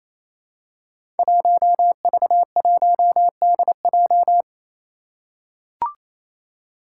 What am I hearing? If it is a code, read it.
Morse code, heard as a keyed tone, about 29 wpm, 700 Hz. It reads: 1V1DJ
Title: Five Challenging Letter-Number Combinations 28wpm